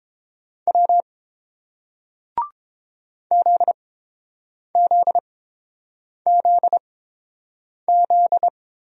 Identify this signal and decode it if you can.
Morse code, heard as a keyed tone, about 31 wpm, spaced as for 7 wpm, 700 Hz. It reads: W ZZZZ